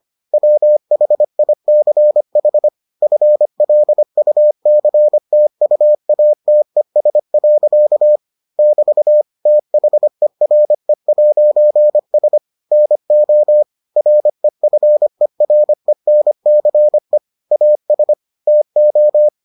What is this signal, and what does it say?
Morse code, heard as a keyed tone, about 25 wpm, 600 Hz. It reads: WHICH FLUCTUATES. <BT> THERE'S NO REFERENCE AS TO